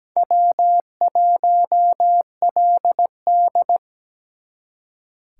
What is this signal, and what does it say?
Morse code, heard as a keyed tone, about 20 wpm, 700 Hz. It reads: W1LD
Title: Speed Racing - USA Call Signs 17wpm